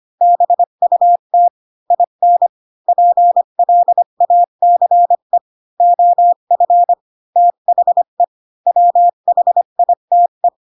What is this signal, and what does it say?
Morse code, heard as a keyed tone, about 25 wpm, 700 Hz. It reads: BUT IN PLACE OF THE WHITE
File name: Pinocchio / Pinocchio ch5 20 25 ik1hge